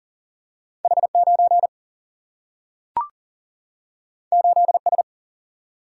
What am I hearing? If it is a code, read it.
Morse code, heard as a keyed tone, about 39 wpm, 700 Hz. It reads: H9 8S
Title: Single-Letter Single-Number 40wpm